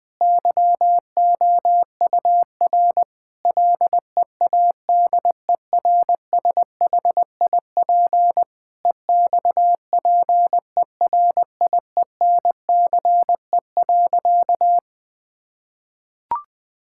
Morse code, heard as a keyed tone, about 20 wpm, 700 Hz. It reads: YOUR LEADERSHIP EXPERIENCE.
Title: Sentences from Top 1600 Words 20wpm